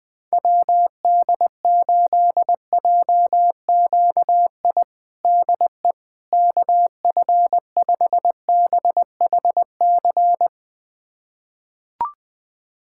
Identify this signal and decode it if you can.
Morse code, heard as a keyed tone, about 20 wpm, 700 Hz. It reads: WD8JQI DE KF5BHC
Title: USA Callsign DE Callsign 20wpm